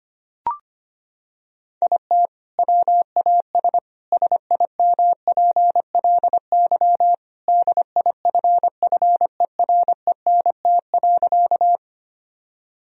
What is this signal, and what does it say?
Morse code, heard as a keyed tone, about 25 wpm, 700 Hz. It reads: IT WAS SIMPLY DIFFERENT.
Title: Sentences from Top 1000 Words - Review Part 4 - 25wpm